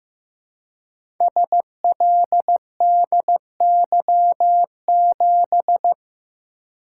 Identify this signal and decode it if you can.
Morse code, heard as a keyed tone, about 15 wpm, 700 Hz. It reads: SLDY7
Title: Code Groups - 5 Letter:Numbers - 15wpm